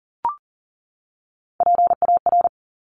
Morse code, heard as a keyed tone, about 41 wpm, 690 Hz. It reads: PAR
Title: Top Three Letter Combinations in Words 40wpm